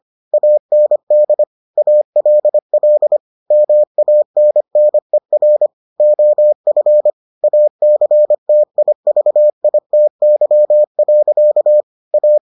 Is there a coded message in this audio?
Yes: Morse code from 0.3 to 12.4 s, heard as a keyed tone, about 25 wpm, 600 Hz. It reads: AND ALL MANNER OF ACTIVITY. A